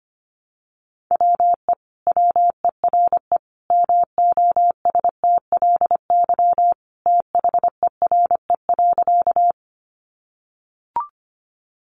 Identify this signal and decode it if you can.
Morse code, heard as a keyed tone, about 25 wpm, 700 Hz. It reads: WE WERE MOSTLY THERE.
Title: Sentences from Top 1300 Words 25wpm